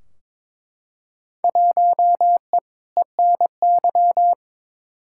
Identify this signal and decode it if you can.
Morse code, heard as a keyed tone, about 22 wpm, 700 Hz. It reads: 1E ENY
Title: Field Day Exchanges - 22wpm